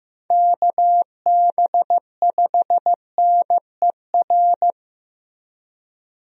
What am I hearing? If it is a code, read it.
Morse code, heard as a keyed tone, about 15 wpm, 700 Hz. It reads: KB5NER
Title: USA Call Signs - 6 Characters - Repeated 2x - 15wpm